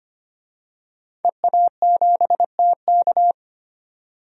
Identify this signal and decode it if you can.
Morse code, heard as a keyed tone, about 25 wpm, 700 Hz. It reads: EA7TK
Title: DXCC-100 Call Signs - 5 Characters - 25wpm